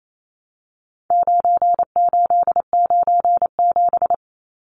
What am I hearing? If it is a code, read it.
Morse code, heard as a keyed tone, about 28 wpm, 700 Hz. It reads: 9897
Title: Numbers - 4-Digit 28wpm